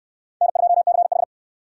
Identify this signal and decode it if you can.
Morse code, heard as a keyed tone, about 69 wpm, 700 Hz. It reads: N2CH